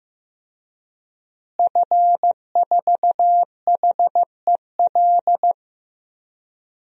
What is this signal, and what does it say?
Morse code, heard as a keyed tone, about 15 wpm, 700 Hz. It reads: F4HEL
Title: DXCC-100 Call Signs - 5 Characters - Repeated 3x - 15wpm